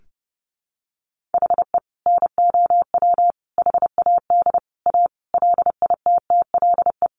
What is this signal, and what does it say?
Morse code, heard as a keyed tone, about 30 wpm, 700 Hz. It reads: HE NOW HAD A LITTLE
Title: Sentences from Top 1500 Words 30wpm